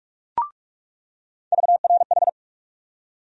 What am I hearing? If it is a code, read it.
Morse code, heard as a keyed tone, about 46 wpm, 700 Hz. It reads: VRH
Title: USA Call Sign Postfixes - 3 Letters 45wpm